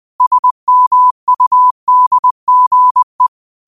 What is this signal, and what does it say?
Morse code, heard as a keyed tone, about 20 wpm, 1000 Hz. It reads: SMUDGE
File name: qotdfiles / 20wpm